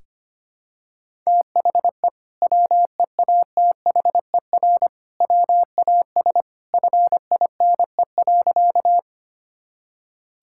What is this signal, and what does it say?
Morse code, heard as a keyed tone, about 25 wpm, 700 Hz. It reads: THE WEATHER WAS FINE.
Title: Sentences from Top 1000 Words - Review Part 5 - 25wpm